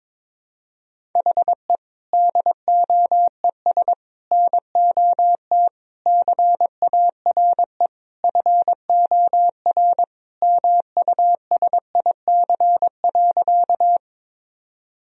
Morse code, heard as a keyed tone, about 22 wpm, 700 Hz. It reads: HE DOES NOT CARE FOR MUSIC.
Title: Sentences from Top 1000 Words - Review Part 1 - 22wpm